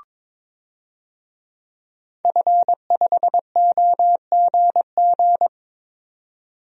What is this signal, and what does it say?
Morse code, heard as a keyed tone, about 22 wpm, 700 Hz. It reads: F5OGG
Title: DXCC-100 Call Signs - 5 Characters - 22wpm